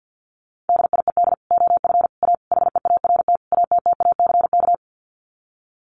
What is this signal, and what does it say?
Morse code, heard as a keyed tone, about 50 wpm, 700 Hz. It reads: DIED OF A HEART ATTACK
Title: Sets of 5 Words Encore2 50wpm